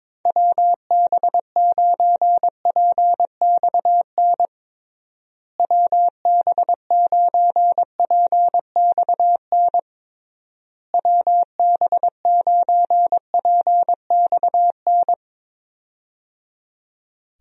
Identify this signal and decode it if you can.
Morse code, heard as a keyed tone, about 22 wpm, 700 Hz. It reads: WB9PXN WB9PXN WB9PXN
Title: USA Call Signs - 6 Characters - Repeated 3x - 22wpm